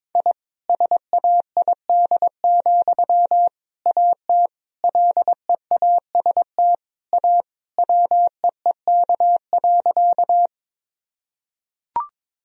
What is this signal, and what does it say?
Morse code, heard as a keyed tone, about 22 wpm, 700 Hz. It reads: I SAID, AT LEAST A WEEK.
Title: Sentences from Top 1000 Words - Review Part 2 - 22wpm